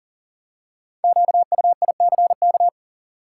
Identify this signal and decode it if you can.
Morse code, heard as a keyed tone, about 40 wpm, 700 Hz. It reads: QUICK
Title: Five Letter Words - 40wpm